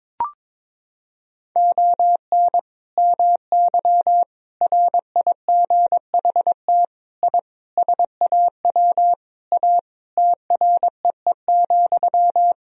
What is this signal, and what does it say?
Morse code, heard as a keyed tone, about 22 wpm, 700 Hz. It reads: ON MY RIGHT I SAW A TREE,
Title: Sentences from Top 1000 Words - Review Part 2 - 22wpm